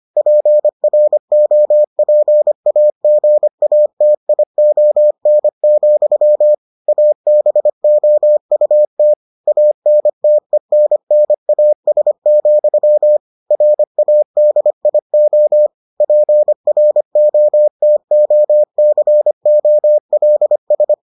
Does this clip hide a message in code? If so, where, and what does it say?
Yes: Morse code from 0.2 to 20.9 s, heard as a keyed tone, about 25 wpm, 600 Hz. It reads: PROPAGATION, ABOUT ANTENNAS, RADIO PROTOCOLS